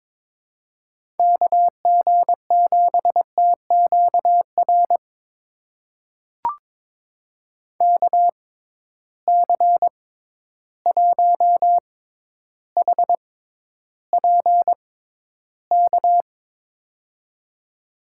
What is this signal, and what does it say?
Morse code, heard as a keyed tone, about 22 wpm, spaced as for 11 wpm, 700 Hz. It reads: KG7TQR KC1HPK
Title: USA Call Signs - 6 Characters - 6x Character Spacing - 22wpm